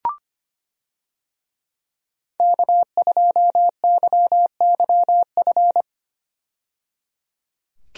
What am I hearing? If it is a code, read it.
Morse code, heard as a keyed tone, about 25 wpm, 690 Hz. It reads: K2YYF
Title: USA Call Signs - 5 Characters - 25wpm